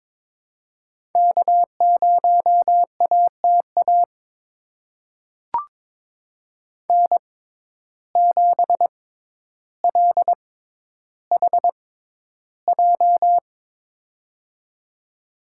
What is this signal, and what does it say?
Morse code, heard as a keyed tone, about 22 wpm, spaced as for 11 wpm, 700 Hz. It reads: K0ATA N7LHJ
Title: USA Call Signs - 5 Characters - 6x Character Spacing - 22wpm